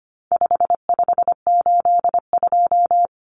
Morse code, heard as a keyed tone, about 25 wpm, 700 Hz. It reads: 5582
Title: Numbers - 4-Digit 25wpm